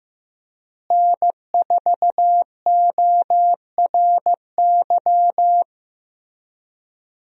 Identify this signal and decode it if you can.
Morse code, heard as a keyed tone, about 15 wpm, 700 Hz. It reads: N4ORY
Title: USA Call Signs - 5 Characters - Repeated 2x - 15wpm